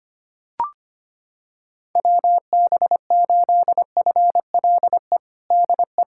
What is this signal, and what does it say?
Morse code, heard as a keyed tone, about 25 wpm, 700 Hz. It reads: WB8FLE DE
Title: USA Callsign DE Callsign - 6 Characters - 25wpm